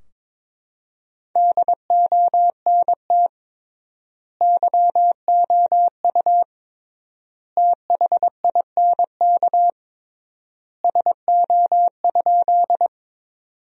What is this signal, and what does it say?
Morse code, heard as a keyed tone, about 22 wpm, 700 Hz. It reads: DONT YOU THINK SO?
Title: 3x Word-Spacing - Sentences from Top 300 Words - 22wpm